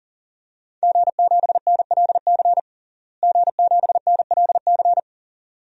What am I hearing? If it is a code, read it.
Morse code, heard as a keyed tone, about 40 wpm, 700 Hz. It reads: G7NLC G7NLC
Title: DXCC-100 Call Signs - 5 Characters - Repeated 2x - 40wpm